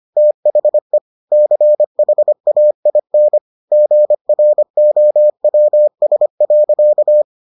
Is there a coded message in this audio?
Yes: Morse code from 0.2 to 7.2 s, heard as a keyed tone, about 25 wpm, 600 Hz. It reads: THE CHAIN GROWS.